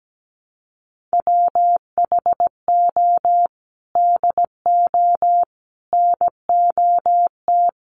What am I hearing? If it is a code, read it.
Morse code, heard as a keyed tone, about 17 wpm, 700 Hz. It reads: WHO DO NOT
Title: Sets of 5 Words Encore2 17wpm